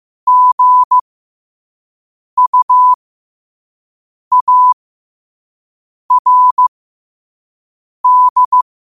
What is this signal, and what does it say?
Morse code, heard as a keyed tone, about 15 wpm, spaced as for 5 wpm, 1000 Hz. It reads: GUARD
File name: qotdfiles / 5wpm